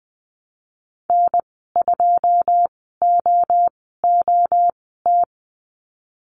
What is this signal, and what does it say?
Morse code, heard as a keyed tone, about 19 wpm, spaced as for 14 wpm, 700 Hz. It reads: N2OOT
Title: 088 Lesson 15g - Number 5 - Call Signs 20wpm